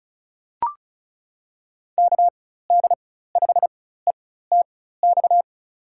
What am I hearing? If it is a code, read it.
Morse code, heard as a keyed tone, about 34 wpm, spaced as for 16 wpm, 690 Hz. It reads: KD5ETX
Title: USA Call Signs - 6 Characters - 4x Character Spacing - 35wpm